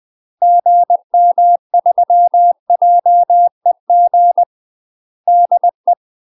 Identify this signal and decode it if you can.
Morse code, heard as a keyed tone, about 20 wpm, 700 Hz. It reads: GM3JEG DE
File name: UK MASTER QSOs TRAINING / 20WPM / UK MASTER QSOs 0024